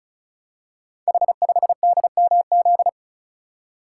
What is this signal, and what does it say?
Morse code, heard as a keyed tone, about 36 wpm, 700 Hz. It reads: H5DMZ
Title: Code Groups - 5 Letter:Numbers - 35wpm